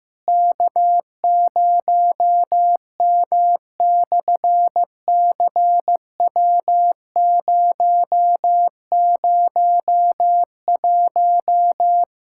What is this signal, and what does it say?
Morse code, heard as a keyed tone, about 15 wpm, 700 Hz. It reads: K0M/CW001
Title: SOTA QSO Practice 15wpm